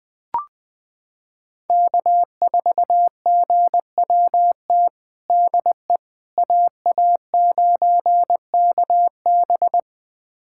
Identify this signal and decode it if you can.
Morse code, heard as a keyed tone, about 20 wpm, 700 Hz. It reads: K4GWT DE AA9KB
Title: USA Callsign DE Callsign 20wpm